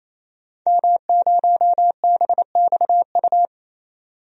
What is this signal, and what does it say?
Morse code, heard as a keyed tone, about 28 wpm, 700 Hz. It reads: M0BXU
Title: DXCC-100 Call Signs - 5 Characters - Repeated 3x - 28wpm